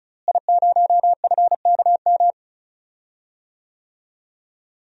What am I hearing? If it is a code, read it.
Morse code, heard as a keyed tone, about 34 wpm, 700 Hz. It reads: I0FKM
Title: DXCC-100 Call Signs - 5 Characters - Repeated 3x - 35wpm